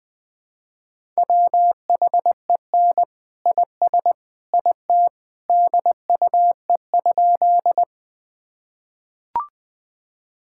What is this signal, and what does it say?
Morse code, heard as a keyed tone, about 20 wpm, 700 Hz. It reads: WHEN IS IT DUE?
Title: Sentences from Top 1000 Words - Review Part 6 - 20wpm